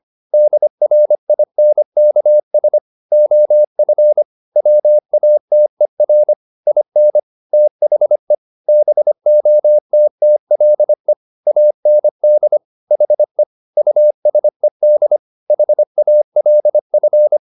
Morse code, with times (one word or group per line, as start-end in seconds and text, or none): 0.3-2.8 DRINKS
3.1-4.2 OF
4.6-6.3 WATER
6.7-7.2 IN
7.5-8.3 THE
8.7-11.1 BOTTLE
11.5-12.6 AND
12.9-13.4 HE
13.8-15.2 USED
15.5-17.4 HALF